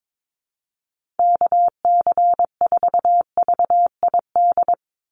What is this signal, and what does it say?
Morse code, heard as a keyed tone, about 22 wpm, 700 Hz. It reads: KC4VID
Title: USA Call Signs 22wpm